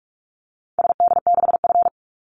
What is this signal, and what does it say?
Morse code, heard as a keyed tone, about 44 wpm, 700 Hz. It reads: SD6F
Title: Call Signs Repeated 2x - 45wpm